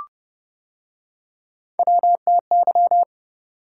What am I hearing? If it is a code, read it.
Morse code, heard as a keyed tone, about 29 wpm, 700 Hz. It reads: WTY